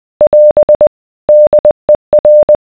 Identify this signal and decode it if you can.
Morse code, heard as a keyed tone, about 20 wpm, 600 Hz. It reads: <AS> DER